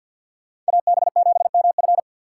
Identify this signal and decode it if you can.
Morse code, heard as a keyed tone, about 63 wpm, 700 Hz. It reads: AB7MF